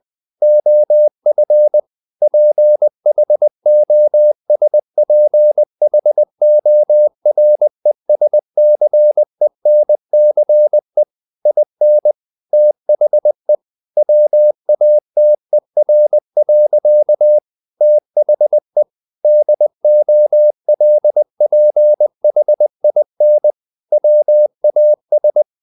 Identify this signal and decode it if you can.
Morse code, heard as a keyed tone, about 20 wpm, 600 Hz. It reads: OF PHOSPHORESCENCE IN THE WATER. THE DOLPHIN WAS
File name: the old man and the sea part 2 20wpm